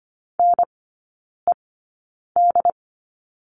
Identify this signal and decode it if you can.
Morse code, heard as a keyed tone, about 25 wpm, spaced as for 9 wpm, 690 Hz. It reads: NED
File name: Lesson 14e - Number 2 - Three Characters 25wpm